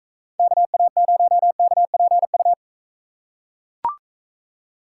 Morse code, heard as a keyed tone, about 39 wpm, 700 Hz. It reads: KA0KPU